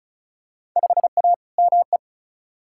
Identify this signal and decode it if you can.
Morse code, heard as a keyed tone, about 36 wpm, 700 Hz. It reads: 5A ME
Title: Field Day Exchanges Encore - 35wpm